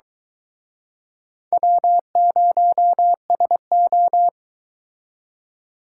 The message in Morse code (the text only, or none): W0SO